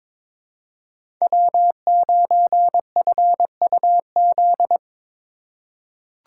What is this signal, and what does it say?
Morse code, heard as a keyed tone, about 22 wpm, 700 Hz. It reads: W9FUZ
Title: USA Call Signs - 2x Character Spacing - 22wpm